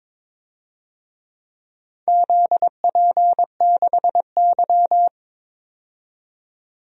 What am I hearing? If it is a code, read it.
Morse code, heard as a keyed tone, about 23 wpm, 700 Hz. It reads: ZP6Y
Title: DXCC-100 Call Signs - 4 Characters - 22wpm